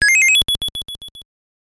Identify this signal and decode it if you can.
There is no code.